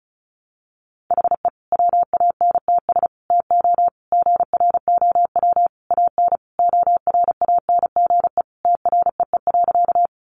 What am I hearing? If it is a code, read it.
Morse code, heard as a keyed tone, about 34 wpm, 700 Hz. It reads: HE WANTS TO GROW AN ORANGE TREE.
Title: Easy Phrases 35wpm